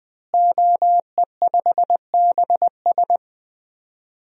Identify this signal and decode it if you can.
Morse code, heard as a keyed tone, about 20 wpm, 700 Hz. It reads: OE5BS
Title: DXCC-100 Call Signs - 5 Characters - Repeated 3x - 20wpm